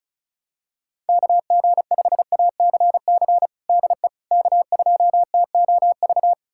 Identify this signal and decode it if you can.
Morse code, heard as a keyed tone, about 34 wpm, 700 Hz. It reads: KG5ACC DE K2TOV
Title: USA Callsign DE Callsign 35wpm